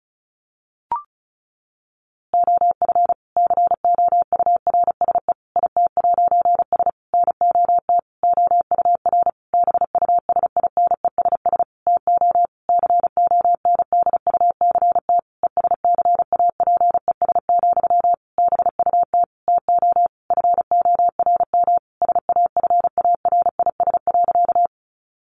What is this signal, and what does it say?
Morse code, heard as a keyed tone, about 34 wpm, 700 Hz. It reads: OF COURSE IT'S NOT OUR BUSINESS TO CONDUCT ESCAPES, BUT TO FORM SAFARIS.
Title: A Sound Of Thunder 35wpm